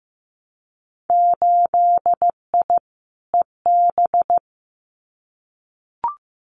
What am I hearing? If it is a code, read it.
Morse code, heard as a keyed tone, about 15 wpm, 700 Hz. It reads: 8I EB
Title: Winter Field Day Exchanges Encore 15wpm